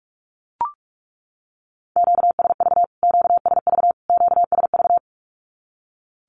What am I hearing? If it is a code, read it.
Morse code, heard as a keyed tone, about 44 wpm, 700 Hz. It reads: QSV QSV QSV
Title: Q-Codes Repeated 3x 45wpm